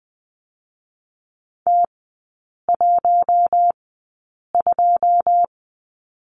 Morse code, heard as a keyed tone, about 20 wpm, spaced as for 8 wpm, 700 Hz. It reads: T12
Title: 081 Lesson 14e - Number 2 - Three Characters 20wpm